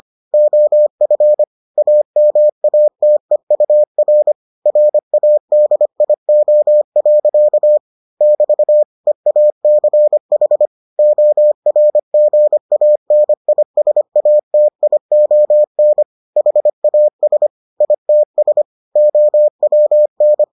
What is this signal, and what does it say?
Morse code, heard as a keyed tone, about 25 wpm, 600 Hz. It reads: OF AMATEUR RADIO. <BT> EACH ORGANISATION HAS ITS OWN